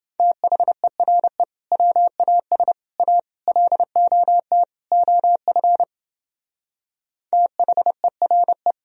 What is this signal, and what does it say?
Morse code, heard as a keyed tone, about 30 wpm, 700 Hz. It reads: THERE WAS A LOT OF THERE